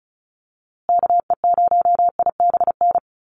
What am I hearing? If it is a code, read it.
Morse code, heard as a keyed tone, about 34 wpm, 700 Hz. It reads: KE0IBN
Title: USA Call Signs Repeated 3x 35wpm